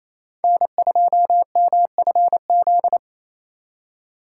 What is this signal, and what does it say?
Morse code, heard as a keyed tone, about 34 wpm, 700 Hz. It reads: N2MFZ
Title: Speed Racing - USA 1x3 Call Signs 28wpm